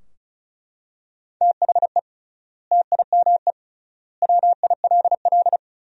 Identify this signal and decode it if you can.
Morse code, heard as a keyed tone, about 36 wpm, 700 Hz. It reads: THE TIME WILL
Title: 3x Word-Spacing - Sentences from Top 100 Words - 35wpm